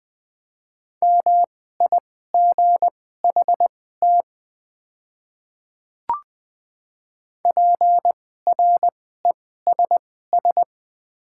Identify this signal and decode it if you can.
Morse code, heard as a keyed tone, about 20 wpm, spaced as for 14 wpm, 700 Hz. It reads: MIGHT PRESS
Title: 2x Character Spacing - Top 300 Words - 20wpm